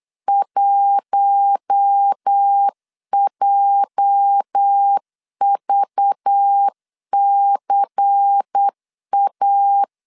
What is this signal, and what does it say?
Morse code, heard as a keyed tone, about 8 wpm, 800 Hz. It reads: 1JVCA
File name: LEZIONI / 8 YJ45 / 8 YJ 8wpm 3313